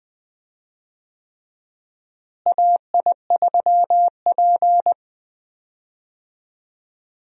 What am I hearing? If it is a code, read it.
Morse code, heard as a keyed tone, about 20 wpm, 700 Hz. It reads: AI3P
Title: USA Call Signs - 4 Characters - Repeated 3x - 20wpm